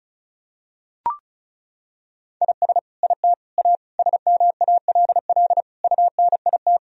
Morse code, heard as a keyed tone, about 36 wpm, 700 Hz. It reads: IS IT A SMALL UNIT
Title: Sentences from Top 1000 Words - Review Part 2 - 35wpm